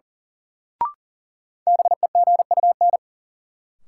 Morse code, heard as a keyed tone, about 38 wpm, 700 Hz. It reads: BEGUN
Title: Sentences from Top 1000 Words - Review Part 6 - New Word Drill - 40wpm